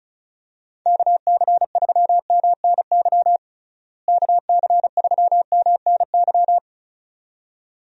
Morse code, heard as a keyed tone, about 36 wpm, 700 Hz. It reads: KC3MNY KC3MNY
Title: USA Call Signs - 6 Characters - Repeated 2x - 35wpm